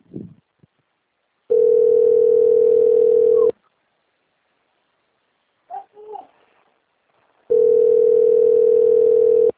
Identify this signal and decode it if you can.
Ringback tone (North America), 440 + 480 Hz.